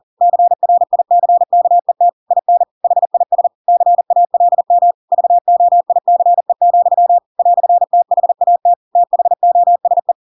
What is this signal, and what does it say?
Morse code, heard as a keyed tone, about 40 wpm, 700 Hz. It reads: CRICKET IN HIS CALM VOICE, "THAT THOSE
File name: Pinocchio / Pinocchio ch4 40 40 ik1hge